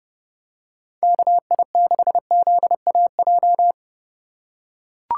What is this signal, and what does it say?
Morse code, heard as a keyed tone, about 30 wpm, 700 Hz. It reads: KI6ZAJ
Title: USA Call Signs - 6 Characters - 4x Character Spacing - 30wpm